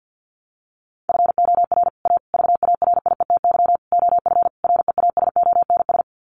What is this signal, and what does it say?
Morse code, heard as a keyed tone, about 50 wpm, 700 Hz. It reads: FOR A VARIETY OF REASONS
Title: Sets of 5 Words Encore 50wpm